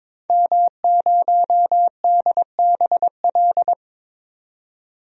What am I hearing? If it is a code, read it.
Morse code, heard as a keyed tone, about 22 wpm, 700 Hz. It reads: M0DBL
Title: DXCC-100 Call Signs - 5 Characters - 22wpm